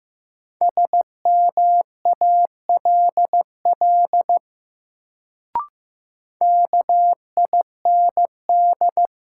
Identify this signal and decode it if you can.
Morse code, heard as a keyed tone, about 15 wpm, 700 Hz. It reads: SMALL KIND
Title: Sentences from Top 200 Words - New Word Drill - 15wpm